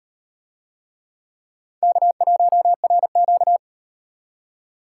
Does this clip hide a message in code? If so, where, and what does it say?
Yes: Morse code from 1.8 to 3.6 s, heard as a keyed tone, about 36 wpm, 700 Hz. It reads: K1RQ